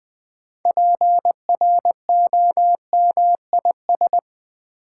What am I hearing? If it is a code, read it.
Morse code, heard as a keyed tone, about 20 wpm, 700 Hz. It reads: PROMIS
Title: Build A Word - 3 Letters At A Time - 9-Letter Words - 20wpm